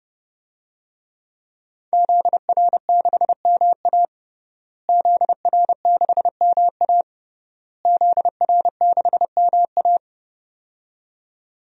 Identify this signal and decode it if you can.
Morse code, heard as a keyed tone, about 30 wpm, 700 Hz. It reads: ZR6MA ZR6MA ZR6MA
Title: DXCC-100 Call Signs - 5 Characters - Repeated 3x - 30wpm